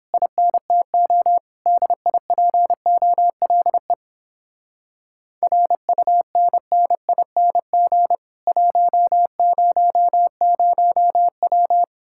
Morse code, heard as a keyed tone, about 31 wpm, 700 Hz. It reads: INTO DIPOLE RUNNING 100W